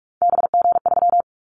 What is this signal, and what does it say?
Morse code, heard as a keyed tone, about 46 wpm, 700 Hz. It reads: BG3
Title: Callsign Prefixes for Top 100 DXCC - 45wpm